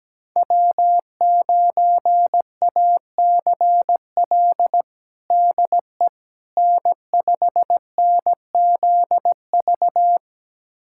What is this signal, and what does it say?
Morse code, heard as a keyed tone, about 17 wpm, 700 Hz. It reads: W9ACL DE N5NZV
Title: USA Callsign DE Callsign - 5 Characters - 17wpm